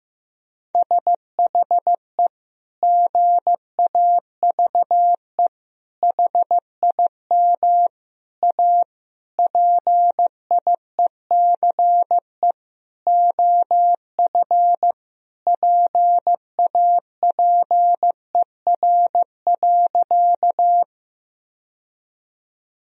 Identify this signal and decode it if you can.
Morse code, heard as a keyed tone, about 16 wpm, 700 Hz. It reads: SHE GAVE HIM A PIECE OF PAPER.
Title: Sentences from Top 1000 Words - Review Part 2 - 15wpm